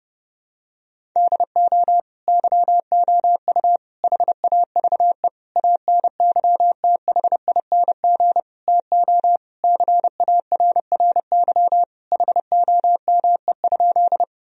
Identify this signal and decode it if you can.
Morse code, heard as a keyed tone, about 30 wpm, 700 Hz. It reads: DO YOU HAVE ANYTHING TO CARRY HOME?
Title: Sentences from Top 1000 Words - Review Part 1 - 30wpm